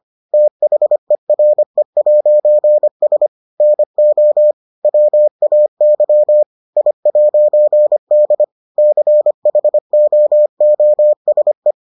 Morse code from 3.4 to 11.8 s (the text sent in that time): NO WAY I'D CHOOSE